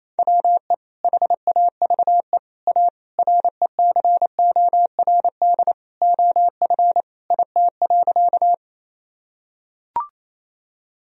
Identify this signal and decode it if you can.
Morse code, heard as a keyed tone, about 27 wpm, 700 Hz. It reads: WE HAVE A RECORD OF IT.
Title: Sentences from Top 1000 Words - Review Part 3 - 28wpm